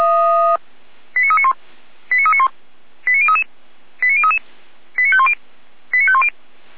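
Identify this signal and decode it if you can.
Five-tone selective call (ZVEI), heard as quick runs of five tones, tones of about 70 ms. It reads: 89391 89391 89033 89033 89510 89510
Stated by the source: ZVEI five-tone call received by SDR.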